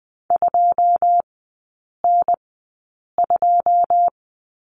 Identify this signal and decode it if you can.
Morse code, heard as a keyed tone, about 20 wpm, spaced as for 8 wpm, 700 Hz. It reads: 2N2
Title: 133 Lesson 22e - Question Mark - Three Characters 20wpm